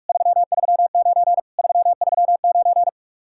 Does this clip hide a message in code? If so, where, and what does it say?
Yes: Morse code from 0.1 to 2.9 s, heard as a keyed tone, about 58 wpm, 700 Hz. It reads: 339 339